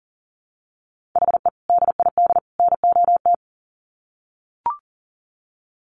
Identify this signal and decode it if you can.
Morse code, heard as a keyed tone, about 40 wpm, 700 Hz. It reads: HE DID NOT
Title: Sets of 3 Words 40wpm